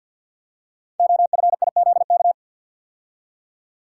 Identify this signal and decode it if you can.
Morse code, heard as a keyed tone, about 53 wpm, 700 Hz. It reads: OFIZK
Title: Code Groups - 5 Letters - 50wpm